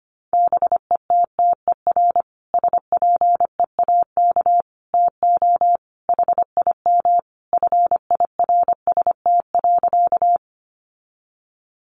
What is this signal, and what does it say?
Morse code, heard as a keyed tone, about 25 wpm, 700 Hz. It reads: BETTER SPEAK TO HIM FIRST.
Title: Sentences from Top 1100 Words 25wpm